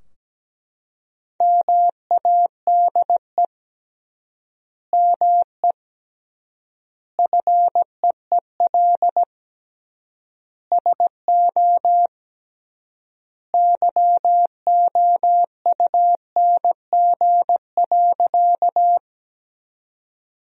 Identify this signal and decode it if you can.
Morse code, heard as a keyed tone, about 17 wpm, 700 Hz. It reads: MADE ME FEEL SO YOUNG.
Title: 3x Word-Spacing - Sentences from Top 400 Words - 17wpm